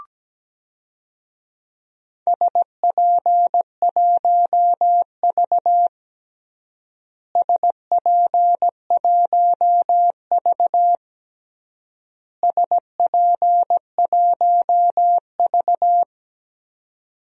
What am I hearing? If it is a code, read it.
Morse code, heard as a keyed tone, about 17 wpm, 700 Hz. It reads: SP1V SP1V SP1V
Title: DXCC-100 Call Signs - 4 Characters - Repeated 3x - 17wpm